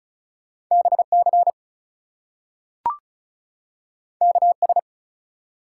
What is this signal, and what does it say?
Morse code, heard as a keyed tone, about 36 wpm, 700 Hz. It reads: BC KS
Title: US States and Canadian Provinces 35wpm